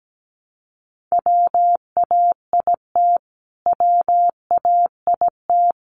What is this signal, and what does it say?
Morse code, heard as a keyed tone, about 17 wpm, 700 Hz. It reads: WAIT WAIT
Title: Top 500 Words Repeated 17wpm